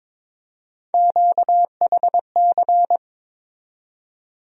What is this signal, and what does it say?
Morse code, heard as a keyed tone, about 22 wpm, 700 Hz. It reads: QHC
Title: USA Call Sign Postfixes - 3 Letters 22wpm